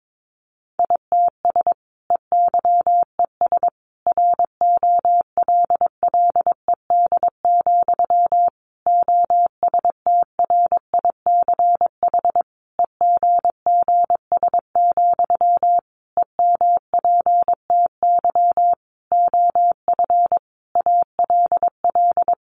Morse code, heard as a keyed tone, about 22 wpm, 700 Hz. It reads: ITS EYES ROLLED, OSTRICH EGGS, EMPTY OF ALL
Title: A Sound Of Thunder 22wpm